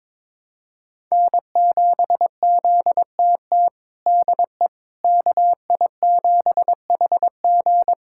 Morse code, heard as a keyed tone, about 22 wpm, 700 Hz. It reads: N7ZTT DE KI7HG
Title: USA Callsign DE Callsign - 5 Characters - 22wpm